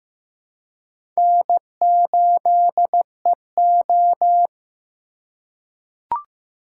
Morse code, heard as a keyed tone, about 15 wpm, 700 Hz. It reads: N8EO
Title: USA Call Signs - 4 Characters - 2x Character Spacing - 15wpm